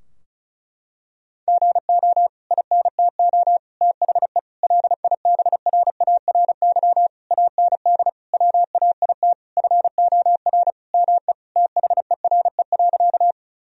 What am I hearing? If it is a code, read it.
Morse code, heard as a keyed tone, about 36 wpm, 700 Hz. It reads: GO INTO THE LIBRARY AND WAIT FOR ME THERE.
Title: Sentences from Top 1000 Words - Review Part 4 - 35wpm